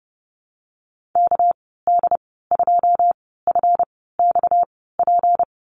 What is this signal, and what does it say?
Morse code, heard as a keyed tone, about 30 wpm, spaced as for 17 wpm, 700 Hz. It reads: KD2FXP
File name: Lesson 37g - Letter x - Call Signs 30wpm